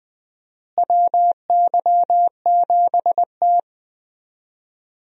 Morse code, heard as a keyed tone, about 20 wpm, 700 Hz. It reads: WY7T
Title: USA Call Signs - 4 Characters - Repeated 2x - 20wpm